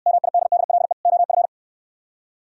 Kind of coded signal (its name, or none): Morse code